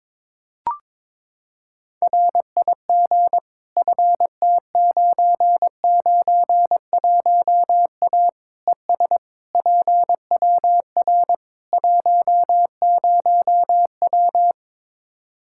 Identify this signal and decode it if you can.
Morse code, heard as a keyed tone, about 22 wpm, 700 Hz. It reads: RIG FT991A ES PWR 10W
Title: Practice QSO - Exchange 2 - Info - 22wpm